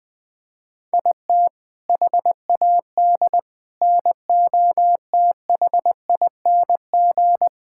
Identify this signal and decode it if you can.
Morse code, heard as a keyed tone, about 21 wpm, 700 Hz. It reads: IT HAD NOTHING
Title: Sentences with 6-Word Phrases - Spoken Beginning - 20wpm